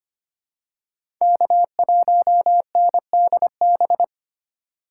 Morse code, heard as a keyed tone, about 25 wpm, 700 Hz. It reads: K1NDB
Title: USA Call Signs - 6x Character Spacing - 25wpm